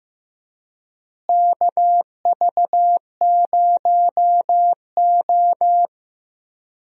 Morse code, heard as a keyed tone, about 15 wpm, 700 Hz. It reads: KV0O
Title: USA Call Signs - 4 Characters - Repeated 2x - 15wpm